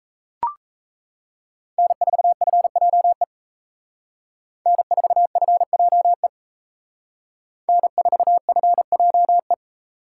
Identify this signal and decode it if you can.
Morse code, heard as a keyed tone, about 32 wpm, 700 Hz. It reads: N4FJE N4FJE N4FJE
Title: Speed Racing - USA 1x3 Call Signs 28wpm